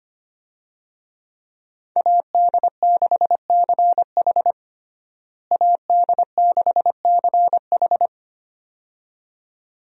Morse code, heard as a keyed tone, about 25 wpm, 700 Hz. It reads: AD6CH AD6CH
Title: USA Call Signs - 5 Characters - Repeated 2x - 25wpm